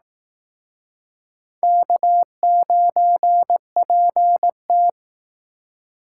Morse code, heard as a keyed tone, about 18 wpm, 700 Hz. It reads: K9PT
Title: Speed Racing - USA 1x2 Call Signs 15wpm